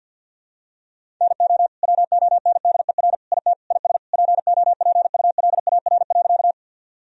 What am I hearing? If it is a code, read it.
Morse code, heard as a keyed tone, about 51 wpm, 700 Hz. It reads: NO WONDER IT IS POPULAR.
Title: Sentences from Top 1000 Words - Review Part 3 - 50wpm